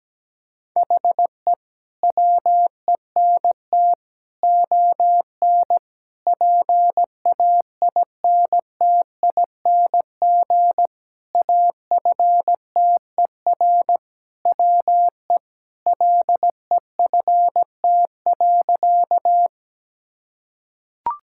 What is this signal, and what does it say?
Morse code, heard as a keyed tone, about 17 wpm, 700 Hz. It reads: HE WENT ON PAINTING AFTER WE LEFT.
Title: Sentences from Top 1000 Words - Review Part 6 - 17wpm